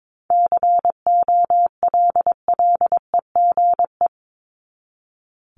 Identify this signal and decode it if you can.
Morse code, heard as a keyed tone, about 22 wpm, 700 Hz. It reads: COLLEGE
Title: Top 1001-1250 Words 22wpm